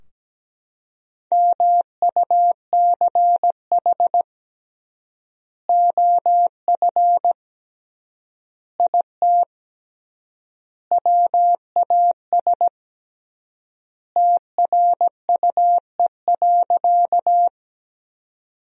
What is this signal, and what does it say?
Morse code, heard as a keyed tone, about 17 wpm, 700 Hz. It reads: MUCH OF IT WAS TRUE.
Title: 3x Word-Spacing - Sentences from Top 400 Words - 17wpm